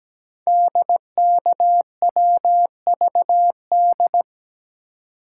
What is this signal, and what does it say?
Morse code, heard as a keyed tone, about 17 wpm, 700 Hz. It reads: DKWVD
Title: Code Groups - 5 Letters - 17wpm